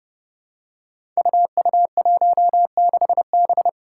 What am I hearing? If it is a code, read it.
Morse code, heard as a keyed tone, about 30 wpm, 700 Hz. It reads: UU16B
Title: Five Challenging Letter-Number Combinations 30wpm